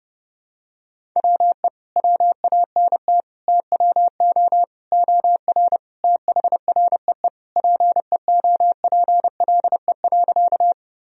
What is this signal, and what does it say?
Morse code, heard as a keyed tone, about 30 wpm, 700 Hz. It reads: WE WANT TWO OR THREE PEOPLE.
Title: Sentences from Top 1000 Words - Review Part 1 - 30wpm